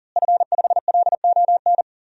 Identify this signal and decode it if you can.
Morse code, heard as a keyed tone, about 39 wpm, 700 Hz. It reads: F5LON
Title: DXCC-100 Call Signs - 5 Characters - Repeated 2x - 40wpm